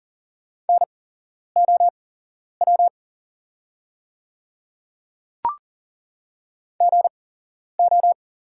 Morse code, heard as a keyed tone, about 37 wpm, spaced as for 11 wpm, 700 Hz. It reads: NOW GO
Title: 8x Character Spacing - Top 100 Words - 40wpm